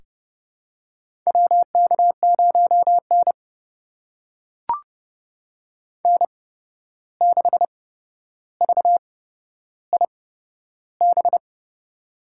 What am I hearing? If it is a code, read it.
Morse code, heard as a keyed tone, about 30 wpm, spaced as for 12 wpm, 700 Hz. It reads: WK0N N6VIB
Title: USA Call Signs - 8x Character Spacing - 30wpm